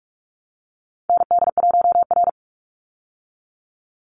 Morse code, heard as a keyed tone, about 44 wpm, 700 Hz. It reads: ND1R
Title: Call Signs Repeated 3x - 45wpm